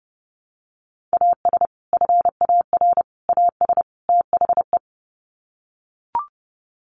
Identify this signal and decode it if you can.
Morse code, heard as a keyed tone, about 30 wpm, 700 Hz. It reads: AS FAR AS THE
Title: Sets of 4 Words 30wpm